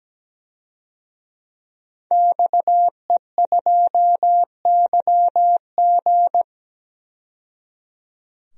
Morse code, heard as a keyed tone, about 17 wpm, 700 Hz. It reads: XE2YG
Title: DXCC-100 Call Signs - 5 Characters - 17wpm